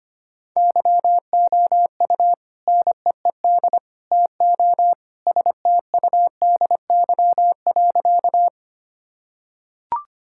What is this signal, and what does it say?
Morse code, heard as a keyed tone, about 25 wpm, 700 Hz. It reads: YOU NEED TO STUDY.
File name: practice / 25wpm / Sentences with 3-Word Phrases - 25wpm